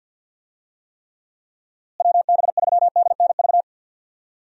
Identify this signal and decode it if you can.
Morse code, heard as a keyed tone, about 52 wpm, 700 Hz. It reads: WB3DNV
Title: USA Call Signs - 6 Characters - Repeated 2x - 50wpm